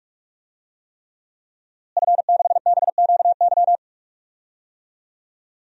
Morse code, heard as a keyed tone, about 41 wpm, 700 Hz. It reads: F6BQY
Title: DXCC-100 Call Signs - 5 Characters - Repeated 3x - 45wpm